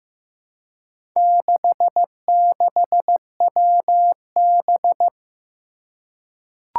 Morse code, heard as a keyed tone, about 15 wpm, 700 Hz. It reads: 66WB
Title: Four Challenging Letter-Number Combinations 15wpm